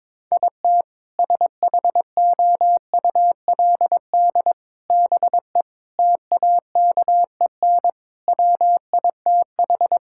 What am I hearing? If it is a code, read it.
Morse code, heard as a keyed tone, about 22 wpm, 700 Hz. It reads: IT SHOULD BE TAKEN WITH
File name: Sentences with 7-Word Phrases - 22wpm